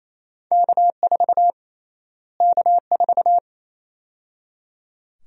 Morse code, heard as a keyed tone, about 28 wpm, 700 Hz. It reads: K4 K4
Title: USA Call Sign Prefixes - Repeated 2x - 28wpm